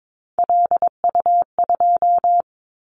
Morse code, heard as a keyed tone, about 22 wpm, 700 Hz. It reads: LU2
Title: Callsign Prefixes for Top 100 DXCC - 22wpm